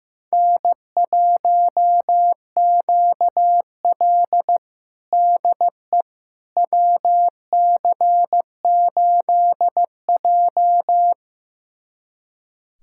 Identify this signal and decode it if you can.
Morse code, heard as a keyed tone, about 15 wpm, 700 Hz. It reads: N1QL DE WC8J
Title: USA Callsign DE Callsign - 4 Characters - 15wpm